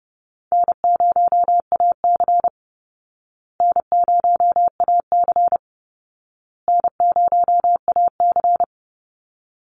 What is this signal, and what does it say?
Morse code, heard as a keyed tone, about 30 wpm, 700 Hz. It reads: N0AC N0AC N0AC
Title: Easy Call Signs Repeated 3x - 30wpm